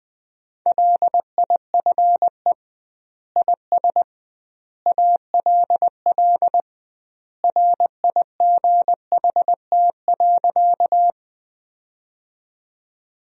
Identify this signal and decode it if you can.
Morse code, heard as a keyed tone, about 20 wpm, 700 Hz. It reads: LIFE IS ALL RIGHT.
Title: 2x Word-Spacing - Sentences from Top 300 Words - 20wpm